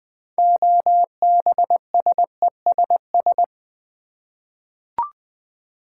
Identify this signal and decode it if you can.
Morse code, heard as a keyed tone, about 20 wpm, 700 Hz. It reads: OBSESS
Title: 6 Letter Words - First 3 Letters Given - 20wpm